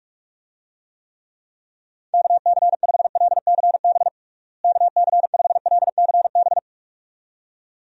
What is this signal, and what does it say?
Morse code, heard as a keyed tone, about 48 wpm, 700 Hz. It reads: KC5LCB KC5LCB
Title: USA Call Signs - 6 Characters - Repeated 2x - 45wpm